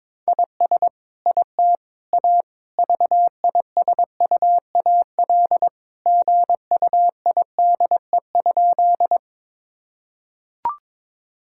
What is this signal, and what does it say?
Morse code, heard as a keyed tone, about 22 wpm, 700 Hz. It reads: IS IT A VISUAL GUIDE?
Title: Sentences from Top 1000 Words - Review Part 6 - 22wpm